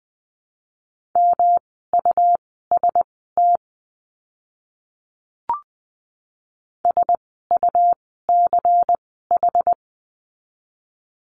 Morse code, heard as a keyed tone, about 20 wpm, spaced as for 14 wpm, 700 Hz. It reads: MUST SUCH
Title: 237 Lesson 38f - Letter q - Words 20wpm